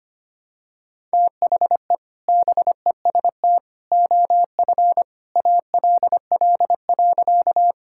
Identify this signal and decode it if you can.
Morse code, heard as a keyed tone, about 25 wpm, 700 Hz. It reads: THE BEST OF ALL.
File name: practice / 25wpm / Sentences with 2-Word Phrases - 25wpm